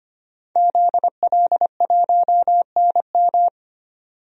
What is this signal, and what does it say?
Morse code, heard as a keyed tone, about 25 wpm, 700 Hz. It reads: ZL1NM
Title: DXCC-100 Call Signs - 5 Characters - 25wpm